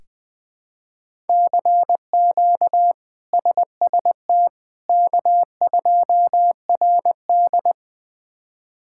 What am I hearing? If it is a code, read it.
Morse code, heard as a keyed tone, about 20 wpm, 700 Hz. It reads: CQ SST K2RD
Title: K1USN SST QSO Practice - 20wpm